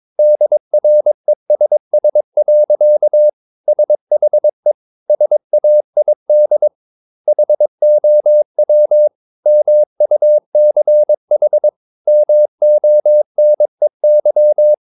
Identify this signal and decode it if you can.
Morse code, heard as a keyed tone, about 22 wpm, 600 Hz. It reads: DRESS. SHE SAID HOW MUCH MONEY
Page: Romanen Never af Ken Follet på engelsk.